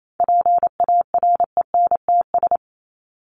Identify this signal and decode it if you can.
Morse code, heard as a keyed tone, about 28 wpm, 700 Hz. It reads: PARENTS
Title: ES:S Suffix - Top 100 Words with Base Words 28wpm